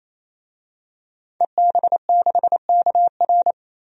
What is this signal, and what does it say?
Morse code, heard as a keyed tone, about 28 wpm, 700 Hz. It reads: EB6KR
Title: DXCC-100 Call Signs - 5 Characters - 28wpm